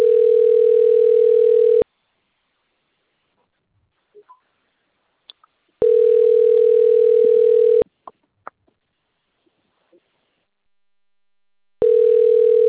Ringback tone (North America), 440 + 480 Hz.